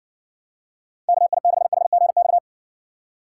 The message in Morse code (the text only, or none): KI6HGX